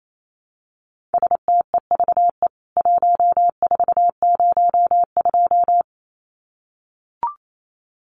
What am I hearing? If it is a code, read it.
Morse code, heard as a keyed tone, about 28 wpm, 700 Hz. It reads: STEVE 1402
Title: CWT Exchange 28wpm